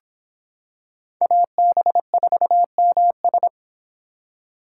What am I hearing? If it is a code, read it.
Morse code, heard as a keyed tone, about 26 wpm, 700 Hz. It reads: AB4MS